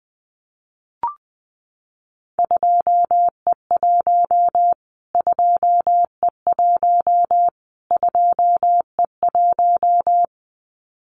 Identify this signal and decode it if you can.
Morse code, heard as a keyed tone, about 20 wpm, 700 Hz. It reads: 2E1 2E1 2E1
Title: Callsign Prefixes for Top 25 DXCC - Repeated 3x - 20wpm